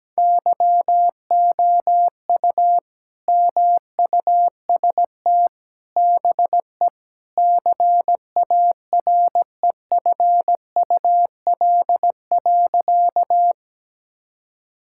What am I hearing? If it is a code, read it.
Morse code, heard as a keyed tone, about 17 wpm, 700 Hz. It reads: YOU MUST BE CAREFUL.
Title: Sentences from Top 1600 Words 17wpm